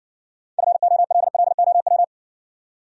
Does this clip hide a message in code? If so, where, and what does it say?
Yes: Morse code from 0.6 to 2.0 s, heard as a keyed tone, about 60 wpm, 700 Hz. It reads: FOLLOW